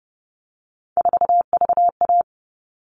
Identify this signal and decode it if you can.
Morse code, heard as a keyed tone, about 30 wpm, 700 Hz. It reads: 4VA
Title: All Cut Numbers - 3-Digit 30wpm